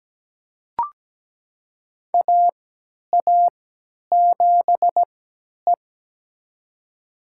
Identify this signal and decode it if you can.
Morse code, heard as a keyed tone, about 17 wpm, spaced as for 10 wpm, 700 Hz. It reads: AA7E
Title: USA Call Signs - 3x Character Spacing - 17wpm